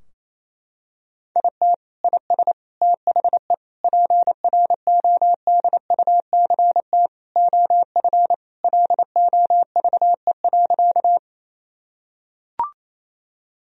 Morse code, heard as a keyed tone, about 28 wpm, 700 Hz. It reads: IT IS THE PRODUCT OF LOVE.
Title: Sentences from Top 1000 Words - Review Part 2 - 28wpm